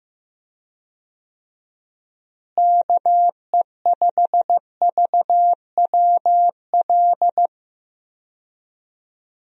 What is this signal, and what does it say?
Morse code, heard as a keyed tone, about 16 wpm, 700 Hz. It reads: KE5VWL
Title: USA Call Signs - 6 Characters - Repeated 2x - 15wpm